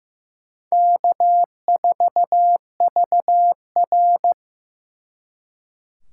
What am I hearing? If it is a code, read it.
Morse code, heard as a keyed tone, about 18 wpm, 700 Hz. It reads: K4VR
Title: Speed Racing - USA Call Signs 15wpm